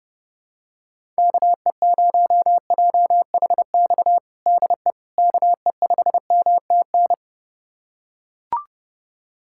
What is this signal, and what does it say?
Morse code, heard as a keyed tone, about 30 wpm, 700 Hz. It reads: KE0JHX DE KE5MTN
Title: USA Callsign DE Callsign - 6 Characters - 30wpm